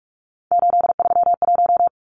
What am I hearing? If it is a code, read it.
Morse code, heard as a keyed tone, about 44 wpm, 700 Hz. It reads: 831
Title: Numbers - 3-Digit 45wpm